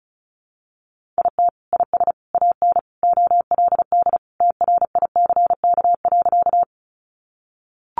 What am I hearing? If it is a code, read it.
Morse code, heard as a keyed tone, about 34 wpm, 700 Hz. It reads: IT IS AN OLD TRICK.
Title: Sentences from Top 1500 Words 35wpm